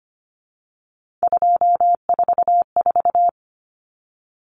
Morse code, heard as a keyed tone, about 25 wpm, 690 Hz. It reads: 244
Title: Numbers - 3-Digit 25wpm